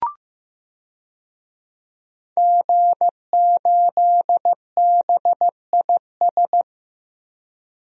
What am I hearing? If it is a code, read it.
Morse code, heard as a keyed tone, about 15 wpm, 700 Hz. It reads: G8BIS
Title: DXCC-100 Call Signs - 5 Characters - 15wpm